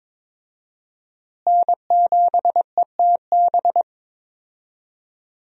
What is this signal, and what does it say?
Morse code, heard as a keyed tone, about 22 wpm, 700 Hz. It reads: N7ETB
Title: USA Call Signs - 5 Characters - Repeated 2x - 22wpm